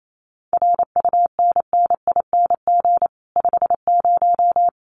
Morse code, heard as a keyed tone, about 28 wpm, 700 Hz. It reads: RUNNING 50
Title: Common QSO Phrases 28wpm